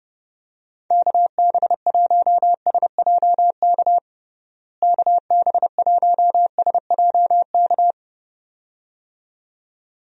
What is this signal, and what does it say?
Morse code, heard as a keyed tone, about 30 wpm, 700 Hz. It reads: KB1SJK KB1SJK
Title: USA Call Signs - 6 Characters - Repeated 2x - 30wpm